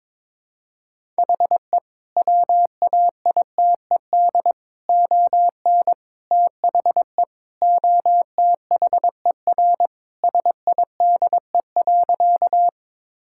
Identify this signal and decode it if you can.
Morse code, heard as a keyed tone, about 22 wpm, 700 Hz. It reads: HE WAITED ON THE OTHER SIDE.
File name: practice / 22wpm / Sentences with 4-Word Phrases - 22wpm